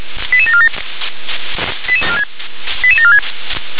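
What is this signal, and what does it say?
Five-tone selective call (ZVEI), heard as quick runs of five tones, tones of about 70 ms. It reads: 88747 88746
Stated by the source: ZVEI five-tone selective call received on SDR.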